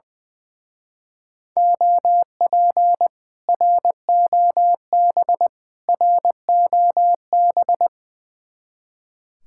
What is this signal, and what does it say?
Morse code, heard as a keyed tone, about 21 wpm, 700 Hz. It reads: OP ROB ROB
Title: Speed-Racing Common QSO Phrases Encore - 20wpm